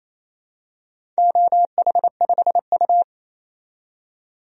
Morse code, heard as a keyed tone, about 28 wpm, 700 Hz. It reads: OH5U
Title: DXCC-100 Call Signs - 4 Characters - 28wpm